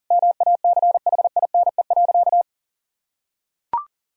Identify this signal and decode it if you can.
Morse code, heard as a keyed tone, about 40 wpm, 700 Hz. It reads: MACHINE.
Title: Sentences from Top 1000 Words - Review Part 2 - 40wpm